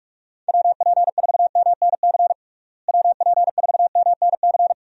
Morse code, heard as a keyed tone, about 41 wpm, 700 Hz. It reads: WP4MNC WP4MNC
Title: USA Call Signs - 6 Characters - Repeated 2x - 45wpm